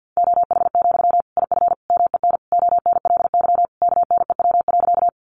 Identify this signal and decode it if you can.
Morse code, heard as a keyed tone, about 51 wpm, 700 Hz. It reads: OH, IF MEN ONLY KNEW.
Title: Sentences from Top 500 Words 50wpm